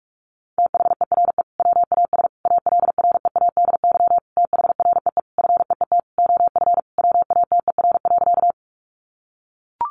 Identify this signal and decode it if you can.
Morse code, heard as a keyed tone, about 44 wpm, 700 Hz. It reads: THERE WAS ALREADY THREE FEET OF WATER.
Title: Sentences from Top 700 Words 45wpm